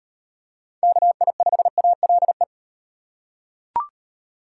Morse code, heard as a keyed tone, about 38 wpm, 700 Hz. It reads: KI5ALE